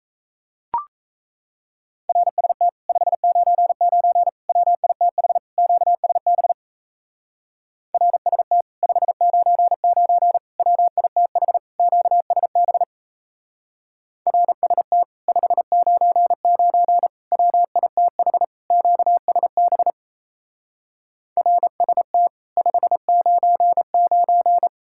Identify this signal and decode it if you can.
Morse code, heard as a keyed tone, about 37 wpm, 700 Hz. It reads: RST 599 WITH QSB RST 599 WITH QSB RST 599 WITH QSB RST 599